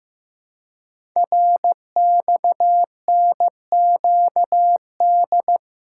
Morse code, heard as a keyed tone, about 15 wpm, 700 Hz. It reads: RXNQD
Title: Code Groups - 5 Letters - 15wpm